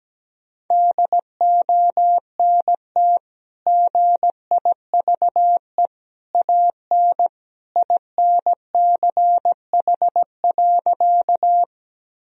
Morse code, heard as a keyed tone, about 17 wpm, 700 Hz. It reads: DONT GIVE AN INCH.
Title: Sentences from Top 1000 Words - Review Part 3 - 17wpm